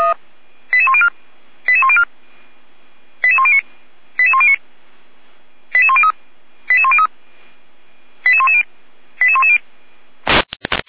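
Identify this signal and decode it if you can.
Five-tone selective call (ZVEI), heard as quick runs of five tones, tones of about 70 ms. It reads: 88104 88104 88109 88109 88103 88103 88110 88110
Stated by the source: ZVEI five-tone selective call received with an SDR.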